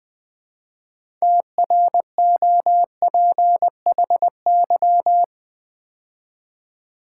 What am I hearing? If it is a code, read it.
Morse code, heard as a keyed tone, about 21 wpm, 700 Hz. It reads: TROPHY
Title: 6 Letter Words - First 3 Letters Given - 20wpm